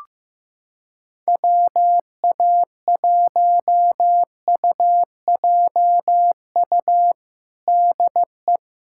Morse code, heard as a keyed tone, about 15 wpm, 700 Hz. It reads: WA1UJU DE
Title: USA Callsign DE Callsign 15wpm